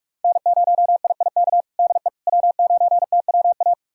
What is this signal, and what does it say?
Morse code, heard as a keyed tone, about 42 wpm, 700 Hz. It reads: N0IIK DE W9TWA
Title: USA Callsign DE Callsign - 5 Characters - 45wpm